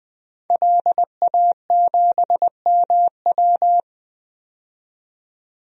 Morse code, heard as a keyed tone, about 20 wpm, 700 Hz. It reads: LA7MW
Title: DXCC-100 Call Signs - 5 Characters - 20wpm